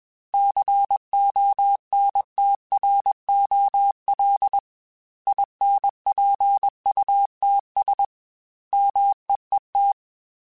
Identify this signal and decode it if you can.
Morse code, heard as a keyed tone, about 21 wpm, 800 Hz. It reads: CONTROL INPUTS MEET
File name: Technical Subjects / Technical Subjects-mp3s / OR Gate @20